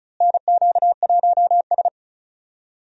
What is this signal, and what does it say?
Morse code, heard as a keyed tone, about 36 wpm, 700 Hz. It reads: NQ1S
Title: USA Call Signs - 4 Characters - Repeated 2x - 35wpm